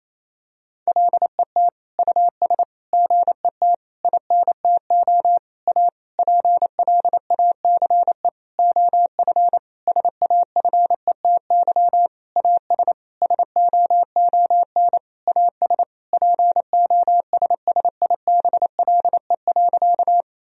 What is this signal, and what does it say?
Morse code, heard as a keyed tone, about 28 wpm, 700 Hz. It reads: LET US GET INTO A PLACE OF SAFETY AS SOON AS POSSIBLE.
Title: Sentences from Top 1000 Words - Review Part 4 - 28wpm